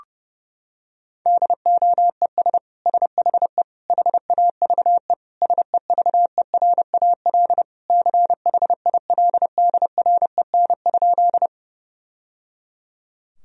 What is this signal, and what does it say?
Morse code, heard as a keyed tone, about 30 wpm, 700 Hz. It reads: DOES SHE HAVE SEVERAL CHILDREN?
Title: Sentences from Top 1000 Words - Review Part 2 - 30wpm